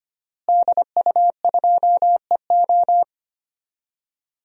Morse code, heard as a keyed tone, about 25 wpm, 700 Hz. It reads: DU2EO
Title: DXCC-100 Call Signs - 5 Characters - Repeated 3x - 25wpm